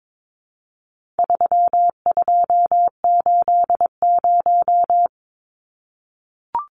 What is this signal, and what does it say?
Morse code, heard as a keyed tone, about 22 wpm, 700 Hz. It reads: 3280
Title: Numbers - 4-Digit 22wpm